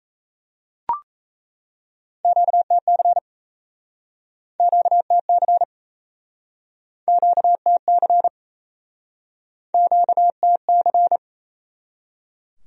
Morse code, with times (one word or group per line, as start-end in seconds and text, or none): 2.2-3.2 QTC
4.6-5.6 QTC
7.1-8.3 QTC
9.7-11.2 QTC